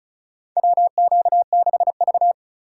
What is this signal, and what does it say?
Morse code, heard as a keyed tone, about 36 wpm, 700 Hz. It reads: WQ6V
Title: USA Call Signs - 4 Characters - 3x Character Spacing - 35wpm